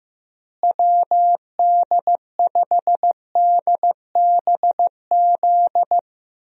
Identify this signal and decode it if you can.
Morse code, heard as a keyed tone, about 15 wpm, 700 Hz. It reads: WD5DBZ
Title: USA Call Signs - 6 Characters - Repeated 2x - 15wpm